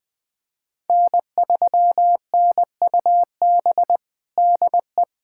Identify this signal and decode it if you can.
Morse code, heard as a keyed tone, about 20 wpm, 700 Hz. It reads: N3NUB DE
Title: USA Callsign DE Callsign - 5 Characters - 20wpm